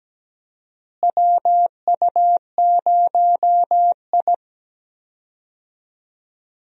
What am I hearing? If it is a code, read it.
Morse code, heard as a keyed tone, about 17 wpm, 700 Hz. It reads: WU0I
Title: USA Call Signs - 4 Characters - Repeated 3x - 17wpm